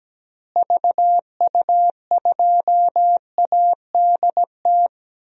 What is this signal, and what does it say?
Morse code, heard as a keyed tone, about 17 wpm, 700 Hz. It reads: VU2ADT
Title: DXCC-100 Call Signs - 6 Characters - Repeated 2x - 17wpm